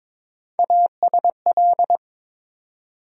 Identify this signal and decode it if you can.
Morse code, heard as a keyed tone, about 22 wpm, 700 Hz. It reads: ASL
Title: USA Call Sign Postfixes - 3 Letters 22wpm